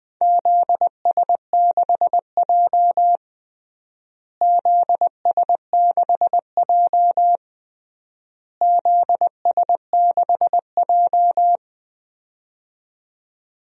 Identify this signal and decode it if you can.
Morse code, heard as a keyed tone, about 20 wpm, 690 Hz. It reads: ZS6J ZS6J ZS6J
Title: DXCC-100 Call Signs - 4 Characters - Repeated 3x - 20wpm